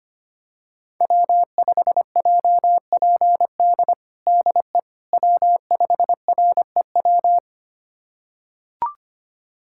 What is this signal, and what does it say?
Morse code, heard as a keyed tone, about 25 wpm, 700 Hz. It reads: W5JPD DE W5REW
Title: USA Callsign DE Callsign - 5 Characters - 25wpm